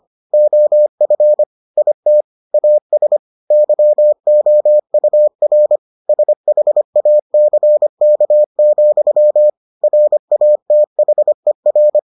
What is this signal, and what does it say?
Morse code, heard as a keyed tone, about 25 wpm, 600 Hz. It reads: OF IT AS YOUR SHACK, RATHER